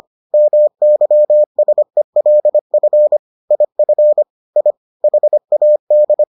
Morse code, heard as a keyed tone, about 25 wpm, 600 Hz. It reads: MYSELF IF I HAD